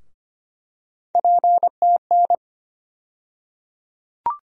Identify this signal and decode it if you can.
Morse code, heard as a keyed tone, about 25 wpm, 700 Hz. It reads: PTN
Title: USA Call Sign Postfixes - 3 Letters 25wpm